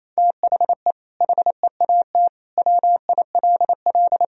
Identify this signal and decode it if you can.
Morse code, heard as a keyed tone, about 28 wpm, 700 Hz. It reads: THE HEAT WILL
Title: Sentences from Top 1000 Words - Review Part 3 - 28wpm